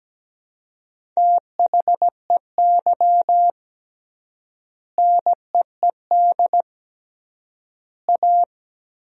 Morse code, heard as a keyed tone, about 17 wpm, 700 Hz. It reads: THEY NEED A
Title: 3x Word-Spacing - Sentences from Top 500 Words - 17wpm